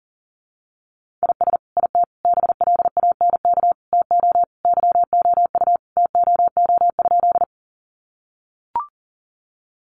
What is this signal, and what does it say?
Morse code, heard as a keyed tone, about 40 wpm, 700 Hz. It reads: IS IT BLANK TO YOU TOO?
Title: Sentences from Top 1500 Words 40wpm